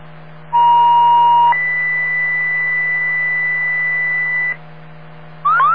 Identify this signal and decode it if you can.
Two-tone page, heard as a one-second tone, then a three-second tone, about 950 then 2000 Hz. It is heard from 0.5 to 4.5 s.